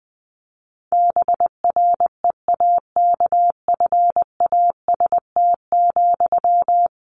Morse code, heard as a keyed tone, about 19 wpm, 700 Hz. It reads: BREAKFAST,
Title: Sentences from Top 1500 Words 20wpm